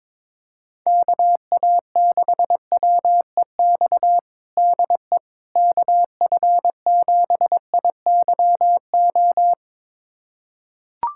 Morse code, heard as a keyed tone, about 22 wpm, 700 Hz. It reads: KA6WEX DE KF7IYO
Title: USA Callsign DE Callsign - 6 Characters - 22wpm